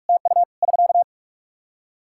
Morse code, heard as a keyed tone, about 46 wpm, 700 Hz. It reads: TU <SK>
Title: Practice QSO - Exchange 4 - Ending - 45wpm